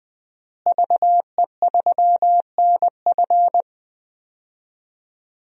Morse code, heard as a keyed tone, about 20 wpm, 700 Hz. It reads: VE3NF
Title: DXCC-100 Call Signs - 5 Characters - Repeated 2x - 20wpm